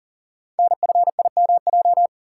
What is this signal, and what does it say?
Morse code, heard as a keyed tone, about 42 wpm, 690 Hz. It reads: NFIMJ
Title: Code Groups - 5 Letters - 40wpm